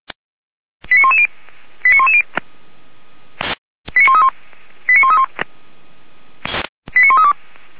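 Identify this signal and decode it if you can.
Five-tone selective call (ZVEI), heard as quick runs of five tones, tones of about 70 ms. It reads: 89110 89110 89131 89131 89132
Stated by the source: Five-tone ZVEI selective call received on SDR.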